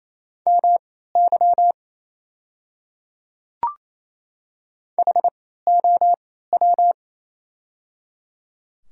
Morse code, heard as a keyed tone, about 28 wpm, spaced as for 16 wpm, 690 Hz. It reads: MY HOW
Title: 3x Character Spacing - Top 100 Words - 28wpm